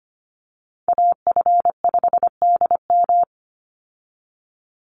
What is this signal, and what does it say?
Morse code, heard as a keyed tone, about 25 wpm, 700 Hz. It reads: AF5DM
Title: Call Signs Repeated 2x - 25wpm